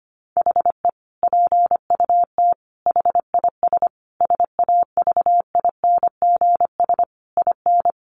Morse code, heard as a keyed tone, about 25 wpm, 700 Hz. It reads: HE PUT HIS SAVINGS IN
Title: Sentences from Top 1200 Words 25wpm